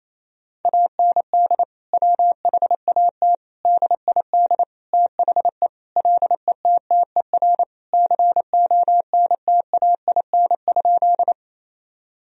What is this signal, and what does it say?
Morse code, heard as a keyed tone, about 28 wpm, 700 Hz. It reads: AND WHAT DID THE LETTER CONTAIN?
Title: Sentences from Top 1000 Words - Review Part 2 - 28wpm